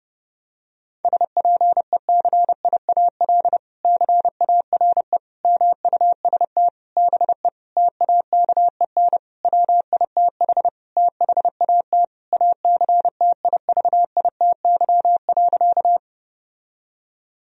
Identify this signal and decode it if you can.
Morse code, heard as a keyed tone, about 32 wpm, 700 Hz. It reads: SPECIAL CARE MUST BE TAKEN WITH THAT ACTIVITY.
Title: Sentences from Top 1000 Words - Review Part 4 - 30wpm